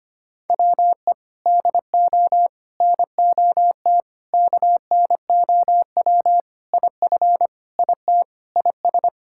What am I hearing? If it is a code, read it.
Morse code, heard as a keyed tone, about 25 wpm, 700 Hz. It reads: WE DO NOT KNOW IF IT IS
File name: practice / 25wpm / Sentences with 4-Word Phrases - 25wpm